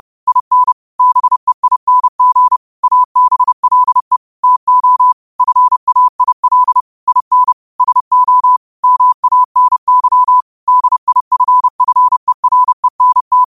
Morse code, heard as a keyed tone, about 30 wpm, 1000 Hz. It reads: IN BEING ABLE TO FAIL IN SO MANY DIFFERENT
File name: qotdfiles / 30wpm